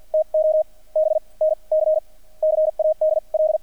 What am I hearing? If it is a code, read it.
Morse code, heard as a keyed tone, about 50 wpm, spaced as for 36 wpm, 620 Hz, with radio noise. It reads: TO BNC CADL
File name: archived / CRSnet